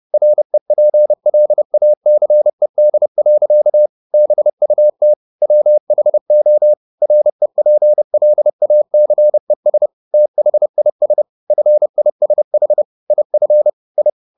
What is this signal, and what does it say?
Morse code, heard as a keyed tone, about 30 wpm, 600 Hz. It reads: REPLACED. BUT WHO REPLACES THIS FISH IF I
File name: the old man and the sea / the old man and the sea 1 30wpm